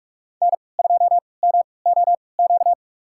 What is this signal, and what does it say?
Morse code, heard as a keyed tone, about 46 wpm, spaced as for 26 wpm, 700 Hz. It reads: N2MOQ
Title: USA Call Signs - 5 Characters - 3x Character Spacing - 45wpm